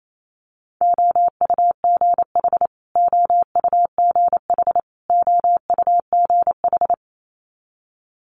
Morse code, heard as a keyed tone, about 28 wpm, 700 Hz. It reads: OUGH OUGH OUGH
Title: Multi-letter Phonemes Repeated 3x 28wpm